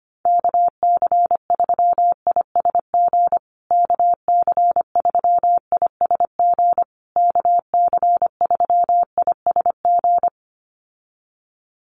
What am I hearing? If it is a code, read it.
Morse code, heard as a keyed tone, about 25 wpm, 700 Hz. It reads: KC3ISG KC3ISG KC3ISG
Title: USA Call Signs Repeated 3x 25wpm